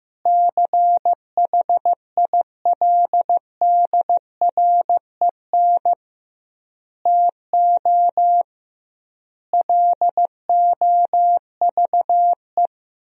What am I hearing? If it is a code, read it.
Morse code, heard as a keyed tone, about 16 wpm, 700 Hz. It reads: CHILDREN TO LOVE
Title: 2x Word-Spacing - Sentences from Top 400 Words - 15wpm